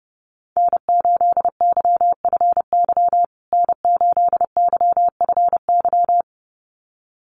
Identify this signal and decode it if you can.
Morse code, heard as a keyed tone, about 30 wpm, 700 Hz. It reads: N8YFY N8YFY
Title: USA Call Signs Repeated 2x 30wpm